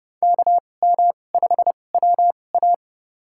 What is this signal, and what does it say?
Morse code, heard as a keyed tone, about 30 wpm, spaced as for 22 wpm, 700 Hz. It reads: KM5WA
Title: USA Call Signs - 5 Characters - 2x Character Spacing - 30wpm